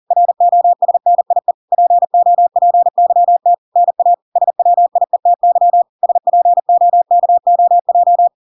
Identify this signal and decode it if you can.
Morse code, heard as a keyed tone, about 40 wpm, 700 Hz. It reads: ROSNIE POPYT NA SWIETY SPOKOJ